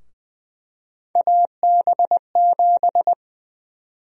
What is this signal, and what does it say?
Morse code, heard as a keyed tone, about 20 wpm, 700 Hz. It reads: AB7
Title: USA Call Sign Prefixes - Repeated 2x - 20wpm